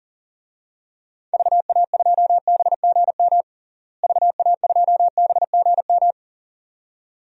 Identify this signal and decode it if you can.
Morse code, heard as a keyed tone, about 40 wpm, 700 Hz. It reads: VA2BGM VA2BGM
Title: DXCC-100 Call Signs - 6 Characters - Repeated 2x - 40wpm